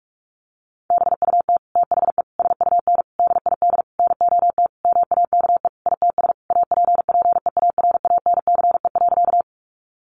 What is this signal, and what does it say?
Morse code, heard as a keyed tone, about 44 wpm, 700 Hz. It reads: BUT THE SUN DID NOT MAKE ITS APPEARANCE.
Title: Sentences from Top 1100 Words 45wpm